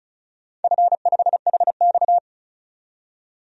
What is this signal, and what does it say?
Morse code, heard as a keyed tone, about 34 wpm, 700 Hz. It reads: F5HX
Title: DXCC-100 Call Signs - 4 Characters - 35wpm